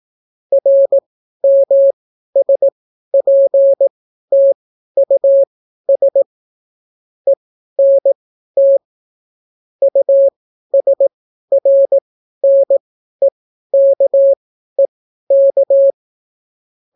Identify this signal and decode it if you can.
Morse code, heard as a keyed tone, about 18 wpm, spaced as for 12 wpm, 550 Hz. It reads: RMSPTUS ENT USRNEKEK